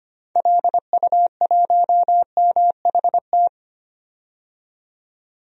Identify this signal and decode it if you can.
Morse code, heard as a keyed tone, about 25 wpm, 700 Hz. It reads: LU1MHT
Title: DXCC-100 Call Signs - 6 Characters - 25wpm